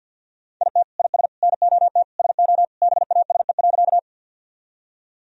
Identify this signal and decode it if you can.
Morse code, heard as a keyed tone, about 51 wpm, 700 Hz. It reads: IT IS NOT SO BASE.
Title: Sentences from Top 1000 Words - Review Part 1 - 50wpm